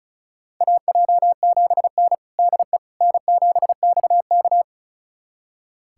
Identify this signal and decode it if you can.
Morse code, heard as a keyed tone, about 34 wpm, 700 Hz. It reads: AJ7N DE N7XK
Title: USA Callsign DE Callsign - 4 Characters - 35wpm